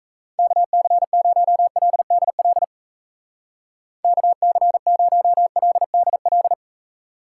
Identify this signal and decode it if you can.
Morse code, heard as a keyed tone, about 34 wpm, 700 Hz. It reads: KC0LDL KC0LDL
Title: Speed Racing - USA Call Signs 28wpm